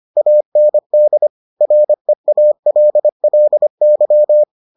Morse code, heard as a keyed tone, about 25 wpm, 600 Hz. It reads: AND REALLY